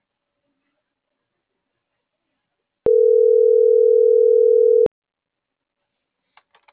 Ringback tone (North America), 440 + 480 Hz.